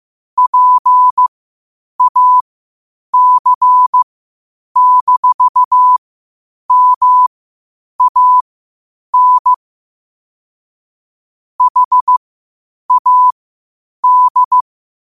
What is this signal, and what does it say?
Morse code, heard as a keyed tone, about 15 wpm, spaced as for 8 wpm, 1000 Hz. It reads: PAC-MAN HAD
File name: qotdfiles / 8wpm